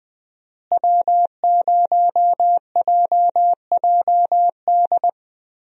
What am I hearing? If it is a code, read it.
Morse code, heard as a keyed tone, about 20 wpm, 700 Hz. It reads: W0JJD
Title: USA Call Signs - 5 Characters - Repeated 3x - 20wpm